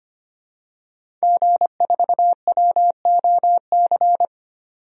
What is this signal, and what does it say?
Morse code, heard as a keyed tone, about 25 wpm, 700 Hz. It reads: G4WOC
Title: DXCC-100 Call Signs - 5 Characters - 25wpm